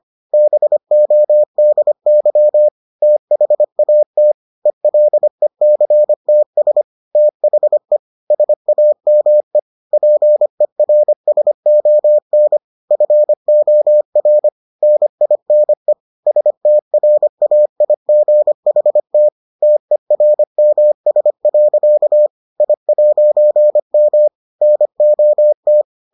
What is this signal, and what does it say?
Morse code, heard as a keyed tone, about 25 wpm, 600 Hz. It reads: BODY THAT ELECTS THE SAME PERSON FOR NINE STRAIGHT TERMS. I'M NOT